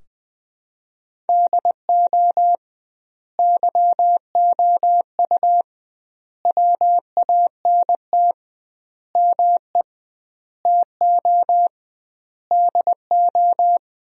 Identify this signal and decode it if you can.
Morse code, heard as a keyed tone, about 21 wpm, 700 Hz. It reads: DO YOU WANT ME TO DO
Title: 2x Word-Spacing - Sentences from Top 200 Words - 20wpm